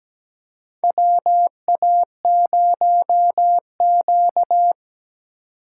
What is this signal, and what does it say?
Morse code, heard as a keyed tone, about 17 wpm, 700 Hz. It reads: WA0Q
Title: USA Call Signs - 3x Character Spacing - 17wpm